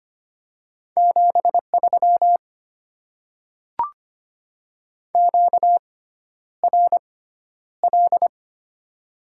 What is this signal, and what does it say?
Morse code, heard as a keyed tone, about 25 wpm, spaced as for 11 wpm, 700 Hz. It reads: 73 QRL
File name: Top 20 QSO Elements - 6x Character Spacing - 25wpm